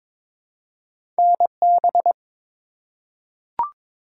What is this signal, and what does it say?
Morse code, heard as a keyed tone, about 22 wpm, 700 Hz. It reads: NB
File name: ARRL Sections v2 - 22wpm